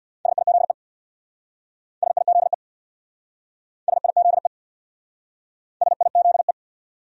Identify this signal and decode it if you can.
Morse code, heard as a keyed tone, about 65 wpm, 690 Hz. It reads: SIZE SIZE SIZE SIZE